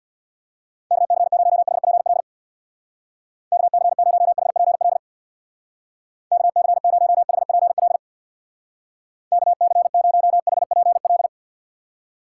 Morse code, heard as a keyed tone, about 62 wpm, 700 Hz. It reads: KC0HPL KC0HPL KC0HPL KC0HPL
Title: Speed Racing - USA 2x3 Call Signs 50wpm